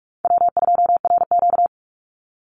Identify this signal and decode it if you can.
Morse code, heard as a keyed tone, about 44 wpm, 700 Hz. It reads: W2RQ
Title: Call Signs - 45wpm